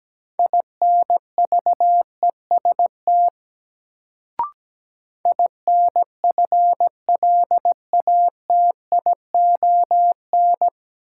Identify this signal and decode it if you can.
Morse code, heard as a keyed tone, about 17 wpm, 700 Hz. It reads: INVEST INFLATION
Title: Sentences from Top 1100 Words - New Word Drill Alt - 17wpm